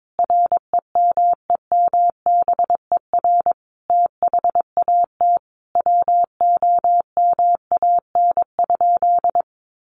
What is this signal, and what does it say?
Morse code, heard as a keyed tone, about 22 wpm, 700 Hz. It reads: REMEMBER THAT WOMAN?
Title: Sentences from Top 1100 Words 22wpm